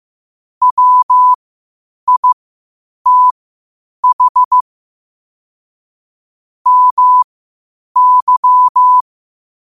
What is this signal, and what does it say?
Morse code, heard as a keyed tone, about 15 wpm, spaced as for 8 wpm, 1000 Hz. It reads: WITH MY
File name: qotdfiles / 8wpm